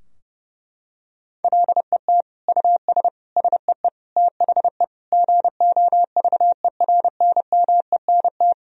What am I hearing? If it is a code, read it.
Morse code, heard as a keyed tone, about 29 wpm, 700 Hz. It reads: LET US SEE THE GOVERNMENT
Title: Sentences from Top 1000 Words - Review Part 3 - 30wpm